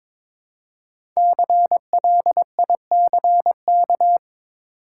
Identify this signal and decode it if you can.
Morse code, heard as a keyed tone, about 22 wpm, 700 Hz. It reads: CLICK
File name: 5 Letter Words - First 3 Letters Given - 22wpm